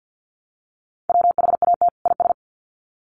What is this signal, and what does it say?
Morse code, heard as a keyed tone, about 50 wpm, 700 Hz. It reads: WHAT IS
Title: Sets of 2 Words 50wpm